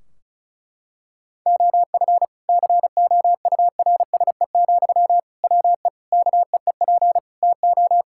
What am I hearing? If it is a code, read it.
Morse code, heard as a keyed tone, about 36 wpm, 700 Hz. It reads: OF COURSE, WE KEEP TO
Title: Sentences from Top 1600 Words 35wpm